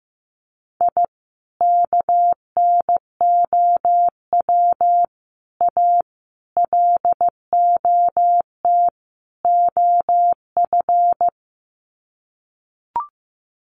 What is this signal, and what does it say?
Morse code, heard as a keyed tone, about 15 wpm, 700 Hz. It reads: I KNOW A LOT OF
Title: Sets of 5 Words Encore 15wpm